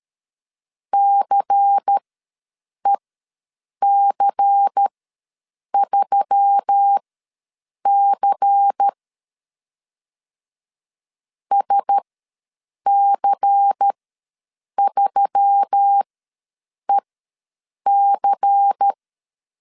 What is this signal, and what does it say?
Morse code, heard as a keyed tone, about 13 wpm, spaced as for 7 wpm, 800 Hz. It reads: CEC3C SC3EC
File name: LEZIONI / 5 LBVC / 5 LBVC ws 12wpm 3159